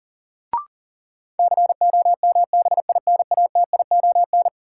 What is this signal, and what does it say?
Morse code, heard as a keyed tone, about 39 wpm, 700 Hz. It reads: COMBINATION
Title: Sentences from Top 1000 Words - Review Part 6 - New Word Drill - 40wpm